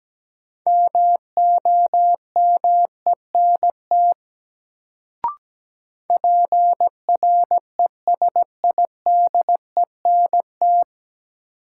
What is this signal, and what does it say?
Morse code, heard as a keyed tone, about 17 wpm, 700 Hz. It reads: MOMENT PRESIDENT
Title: Sentences from Top 1000 Words - New Word Drill - 17wpm